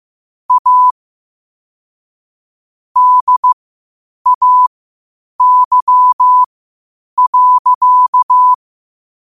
Morse code, heard as a keyed tone, about 15 wpm, spaced as for 8 wpm, 1000 Hz. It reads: A DAY.
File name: qotdfiles / 8wpm